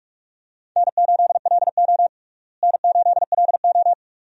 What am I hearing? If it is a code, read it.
Morse code, heard as a keyed tone, about 46 wpm, 700 Hz. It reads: N8LO N8LO
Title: USA Call Signs - 4 Characters - Repeated 2x - 45wpm